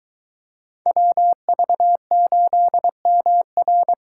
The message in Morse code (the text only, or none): WV8MR